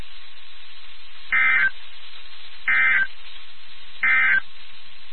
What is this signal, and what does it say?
Emergency Alert System (SAME) tones, heard as harsh buzzing bursts. It is heard from 1.2 to 4.5 s.
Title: NOAA Weather Radio WXK79